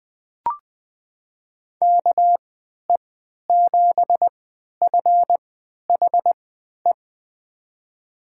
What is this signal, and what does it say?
Morse code, heard as a keyed tone, about 20 wpm, spaced as for 11 wpm, 700 Hz. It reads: KE7FHE
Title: USA Call Signs - 6 Characters - 3x Character Spacing - 20wpm